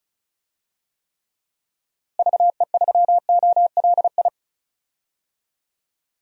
Morse code, heard as a keyed tone, about 36 wpm, 700 Hz. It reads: VE3OLI
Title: DXCC-100 Call Signs - 6 Characters - Repeated 2x - 35wpm